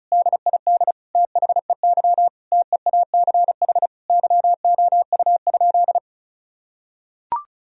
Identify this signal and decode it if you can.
Morse code, heard as a keyed tone, about 36 wpm, 700 Hz. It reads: DID THEY TEACH YOU?
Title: Sentences from Top 1000 Words - Review Part 2 - 35wpm